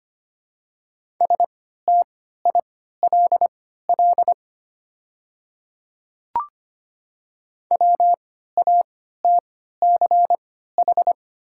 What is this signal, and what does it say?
Morse code, heard as a keyed tone, about 25 wpm, spaced as for 14 wpm, 700 Hz. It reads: STILL WATCH
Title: 3x Character Spacing - Top 500 Words - 25wpm